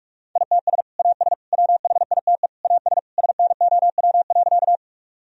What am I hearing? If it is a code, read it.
Morse code, heard as a keyed tone, about 46 wpm, 700 Hz. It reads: ITS AS WHITE AS SNOW.
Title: Sentences from Top 1000 Words - Review Part 3 - 45wpm